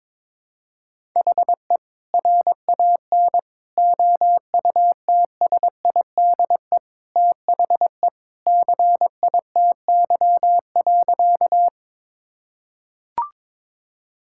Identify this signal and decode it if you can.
Morse code, heard as a keyed tone, about 21 wpm, 700 Hz. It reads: HE RAN OUTSIDE THE CITY.
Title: Sentences from Top 1000 Words - Review Part 6 - 22wpm